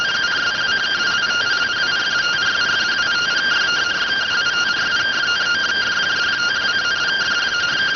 Two-tone teleprinter signal, about 170 Hz shift.